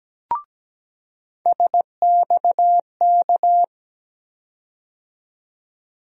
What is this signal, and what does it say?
Morse code, heard as a keyed tone, about 17 wpm, 700 Hz. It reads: SXK
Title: USA Call Sign Postfixes - 3 Letters 17wpm